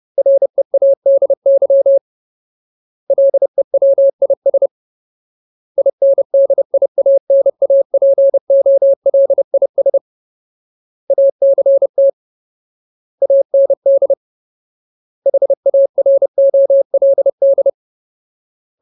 Morse code, heard as a keyed tone, about 30 wpm, 550 Hz. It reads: READY LEWIS INDIANAPOLIS ACT AND HAROLD
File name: Ultimate Word List / 30WPM Ultimate